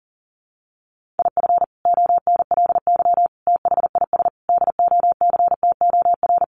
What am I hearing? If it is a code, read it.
Morse code, heard as a keyed tone, about 40 wpm, 700 Hz. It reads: IF ONLY THIS DOCTOR
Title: Sentences from Top 1500 Words 40wpm